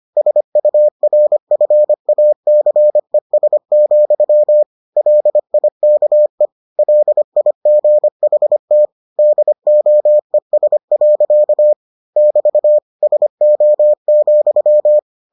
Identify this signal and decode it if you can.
Morse code, heard as a keyed tone, about 25 wpm, 600 Hz. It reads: SURFACES, LIKE LIGHT DOES. <BT> SO,